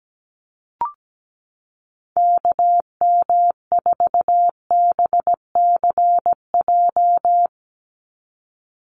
Morse code, heard as a keyed tone, about 17 wpm, 700 Hz. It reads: KM4BCJ
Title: USA Call Signs 17wpm